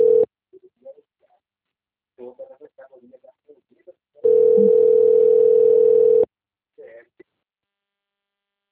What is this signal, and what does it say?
Ringback tone (North America), 440 + 480 Hz, with speech.
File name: rg-1600-unknown- - - .2271